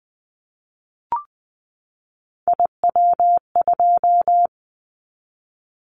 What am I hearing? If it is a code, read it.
Morse code, heard as a keyed tone, about 20 wpm, 700 Hz. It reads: IW2
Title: Callsign Prefixes for Top 50 DXCC - 20wpm